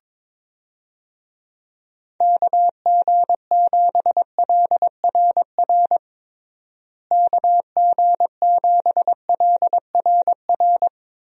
Morse code, heard as a keyed tone, about 22 wpm, 700 Hz. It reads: KG7LRR KG7LRR
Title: USA Call Signs - 6 Characters - Repeated 3x - 22wpm